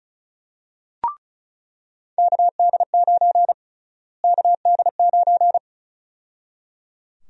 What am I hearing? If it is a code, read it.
Morse code, heard as a keyed tone, about 34 wpm, 700 Hz. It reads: KD9 KD9
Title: USA Call Sign Prefixes - Repeated 2x - 35wpm